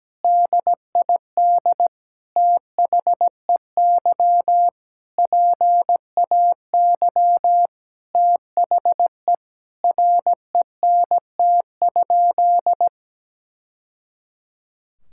Morse code, heard as a keyed tone, about 17 wpm, 700 Hz. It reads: DID THEY PAY THE RENT?
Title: Sentences from Top 1000 Words - Review Part 5 - 17wpm